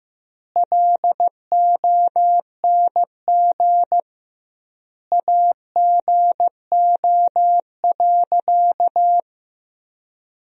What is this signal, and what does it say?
Morse code, heard as a keyed tone, about 15 wpm, 700 Hz. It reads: LONG AGO.
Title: 2x Word-Spacing - Sentences from Top 500 Words - 15wpm